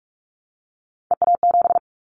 Morse code, heard as a keyed tone, about 44 wpm, 700 Hz. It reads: EA7
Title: Callsign Prefixes for Top 25 DXCC - 45wpm